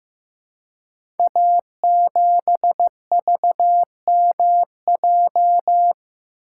Morse code, heard as a keyed tone, about 15 wpm, 700 Hz. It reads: A7VMJ
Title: Code Groups - 5 Letter:Numbers - 15wpm